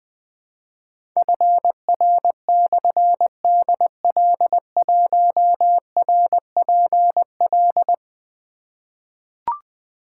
Morse code, heard as a keyed tone, about 20 wpm, 700 Hz. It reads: FR/DL1RPL
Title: Difficult Call Signs 20wpm